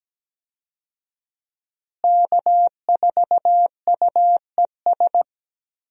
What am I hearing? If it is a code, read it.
Morse code, heard as a keyed tone, about 17 wpm, 700 Hz. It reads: K4UES
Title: USA Call Signs - 5 Characters - Repeated 3x - 17wpm